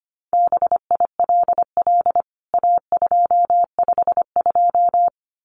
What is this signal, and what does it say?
Morse code, heard as a keyed tone, about 25 wpm, 700 Hz. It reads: BILL A252
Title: CWT Exchange 25wpm